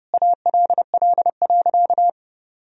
Morse code, heard as a keyed tone, about 30 wpm, 700 Hz. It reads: ALL.
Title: Sentences from Top 1000 Words - Review Part 4 - 30wpm